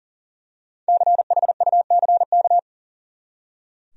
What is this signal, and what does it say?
Morse code, heard as a keyed tone, about 40 wpm, 700 Hz. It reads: CHUCK
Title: USA Common Names - 40wpm